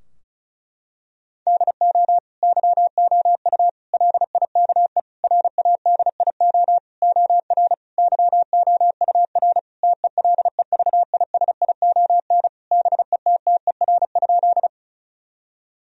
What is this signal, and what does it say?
Morse code, heard as a keyed tone, about 34 wpm, 700 Hz. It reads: DO YOU LIKE RADIO OR YOUR TELEVISION BETTER?
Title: Sentences from Top 1000 Words - Review Part 4 - 35wpm